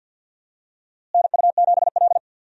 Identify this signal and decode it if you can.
Morse code, heard as a keyed tone, about 51 wpm, 700 Hz. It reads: NU7L
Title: USA Call Signs - 4 Characters - Repeated 3x - 50wpm